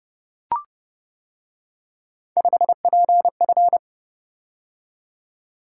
Morse code, heard as a keyed tone, about 30 wpm, 700 Hz. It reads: 5PF